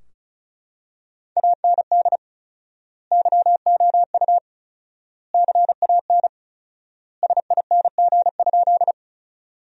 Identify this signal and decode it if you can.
Morse code, heard as a keyed tone, about 36 wpm, 700 Hz. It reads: AND YOU CAN SING?
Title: 4x Word-Spacing - Sentences from Top 400 Words - 35wpm